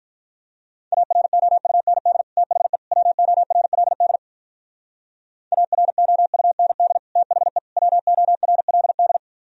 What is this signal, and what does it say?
Morse code, heard as a keyed tone, about 44 wpm, 700 Hz. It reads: AROUND THE WORLD AROUND THE WORLD
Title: Speed-Racing - Sets of 3 Words 35wpm